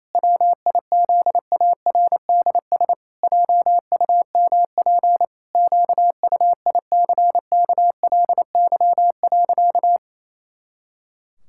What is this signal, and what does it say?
Morse code, heard as a keyed tone, about 28 wpm, 700 Hz. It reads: WIZARDS JUMP QUICKLY.
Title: Pangram Challenge v2 - 28wpm